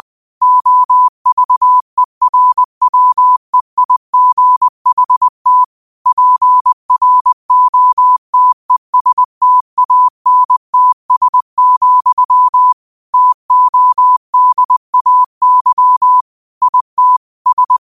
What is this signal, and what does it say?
Morse code, heard as a keyed tone, about 20 wpm, 1000 Hz. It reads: OVERWEIGHT PROTESTANTS, TODAY IT S
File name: qotdfiles / 20wpm